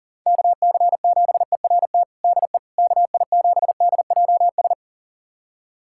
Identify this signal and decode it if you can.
Morse code, heard as a keyed tone, about 37 wpm, 700 Hz. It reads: KC7ERT DE KI7DJS
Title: USA Callsign DE Callsign - 6 Characters - 40wpm